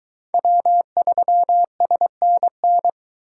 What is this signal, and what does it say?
Morse code, heard as a keyed tone, about 20 wpm, 700 Hz. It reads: W3SNN